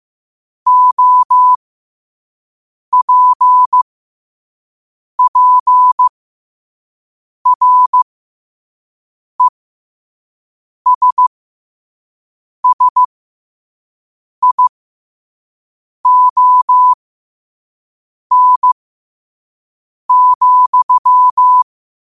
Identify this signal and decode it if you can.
Morse code, heard as a keyed tone, about 15 wpm, spaced as for 5 wpm, 1000 Hz. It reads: OPPRESSION,
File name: qotdfiles / 5wpm